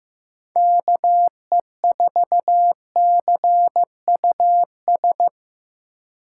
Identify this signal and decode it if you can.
Morse code, heard as a keyed tone, about 15 wpm, 700 Hz. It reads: KE4CUS
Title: USA Call Signs - 2x Character Spacing - 15wpm